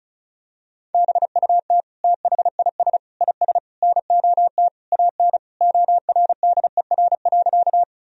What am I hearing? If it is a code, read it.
Morse code, heard as a keyed tone, about 34 wpm, 700 Hz. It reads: BUT THIS IS NOT AN ORDER.
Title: Sentences from Top 1000 Words - Review Part 2 - 35wpm